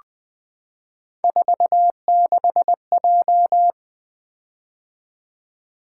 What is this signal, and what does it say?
Morse code, heard as a keyed tone, about 20 wpm, 700 Hz. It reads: 46J
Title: Three Letter-Number Combinations - 20wpm